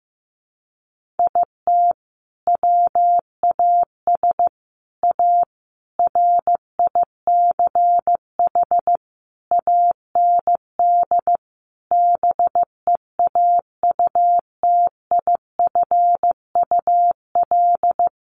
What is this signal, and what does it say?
Morse code, heard as a keyed tone, about 15 wpm, 700 Hz. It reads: IT WAS A RICH AND BEAUTIFUL
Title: Sentences from Top 1100 Words 15wpm